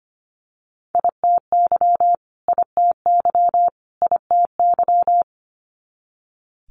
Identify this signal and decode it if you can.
Morse code, heard as a keyed tone, about 25 wpm, 700 Hz. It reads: ITY ITY ITY
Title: Multi-letter Phonemes Repeated 3x 25wpm